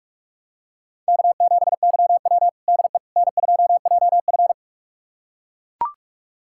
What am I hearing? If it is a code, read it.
Morse code, heard as a keyed tone, about 42 wpm, 700 Hz. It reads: K7YW DE N2JF
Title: USA Callsign DE Callsign - 4 Characters - 45wpm